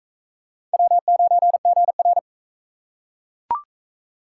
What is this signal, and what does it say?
Morse code, heard as a keyed tone, about 39 wpm, 700 Hz. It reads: W9GR